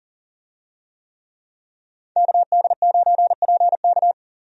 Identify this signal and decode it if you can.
Morse code, heard as a keyed tone, about 40 wpm, 700 Hz. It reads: KD9PK
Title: USA Call Signs - 5 Characters - 40wpm